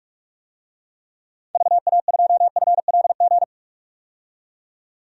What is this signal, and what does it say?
Morse code, heard as a keyed tone, about 44 wpm, 700 Hz. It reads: VA2FLG
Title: DXCC-100 Call Signs - 6 Characters - 45wpm